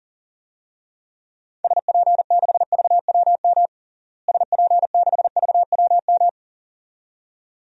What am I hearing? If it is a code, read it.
Morse code, heard as a keyed tone, about 40 wpm, 700 Hz. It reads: SP6VWM SP6VWM
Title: DXCC-100 Call Signs - 6 Characters - Repeated 2x - 40wpm